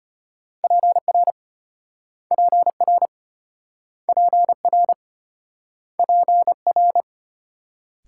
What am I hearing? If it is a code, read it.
Morse code, heard as a keyed tone, about 34 wpm, 700 Hz. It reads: PR PR PR PR